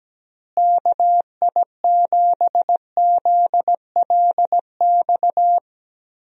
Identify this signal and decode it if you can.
Morse code, heard as a keyed tone, about 17 wpm, 700 Hz. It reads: KI7ZLX
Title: USA Call Signs - 6 Characters - Repeated 2x - 17wpm